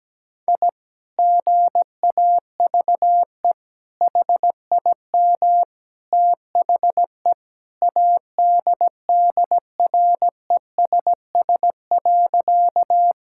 Morse code, heard as a keyed tone, about 17 wpm, 700 Hz. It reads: I GAVE HIM THE ADDRESS.
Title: Sentences from Top 1000 Words - Review Part 6 - 17wpm